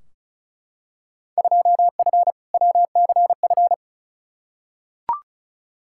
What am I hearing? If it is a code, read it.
Morse code, heard as a keyed tone, about 34 wpm, 700 Hz. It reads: 2F WCF
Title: Field Day Exchanges Encore - 35wpm